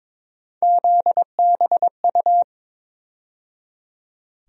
Morse code, heard as a keyed tone, about 22 wpm, 700 Hz. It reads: ZBU
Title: USA Call Sign Postfixes - 3 Letters 22wpm